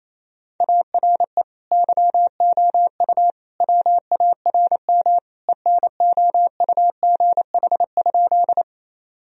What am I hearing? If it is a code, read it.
Morse code, heard as a keyed tone, about 27 wpm, 690 Hz. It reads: ARE YOU WARM ENOUGH?
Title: Sentences from Top 1000 Words - Review Part 2 - 28wpm